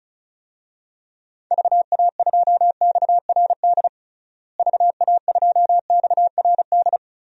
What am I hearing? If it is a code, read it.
Morse code, heard as a keyed tone, about 36 wpm, 700 Hz. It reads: VA2XRD VA2XRD
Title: DXCC-100 Call Signs - 6 Characters - Repeated 3x - 35wpm